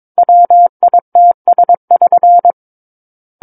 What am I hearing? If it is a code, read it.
Morse code, heard as a keyed tone, about 22 wpm, 700 Hz. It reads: WITS<SN>
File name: Book One / 13-Chapter 13 - 22 WPM